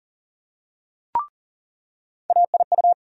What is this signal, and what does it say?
Morse code, heard as a keyed tone, about 40 wpm, 700 Hz. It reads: AIU
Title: USA Call Sign Postfixes - 3 Letters 40wpm